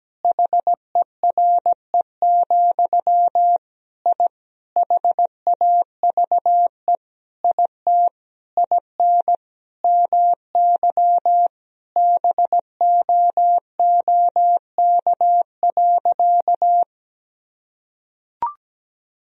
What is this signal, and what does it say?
Morse code, heard as a keyed tone, about 17 wpm, 700 Hz. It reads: HERE, I HAVE IT IN MY BOOK.
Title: Sentences from Top 1000 Words - Review Part 1 - 17wpm